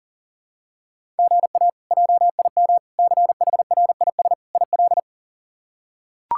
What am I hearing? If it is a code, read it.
Morse code, heard as a keyed tone, about 40 wpm, 700 Hz. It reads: GA JIM CHRIS IL
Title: K1USN SST QSO Practice - 40wpm